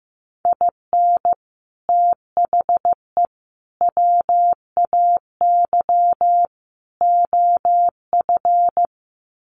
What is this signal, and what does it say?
Morse code, heard as a keyed tone, about 15 wpm, 700 Hz. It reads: IN THE WAY OF
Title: Sets of 4 Words 15wpm